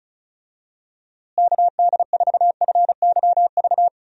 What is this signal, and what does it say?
Morse code, heard as a keyed tone, about 36 wpm, 700 Hz. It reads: KD4FYV
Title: USA Call Signs - 6 Characters - Repeated 3x - 35wpm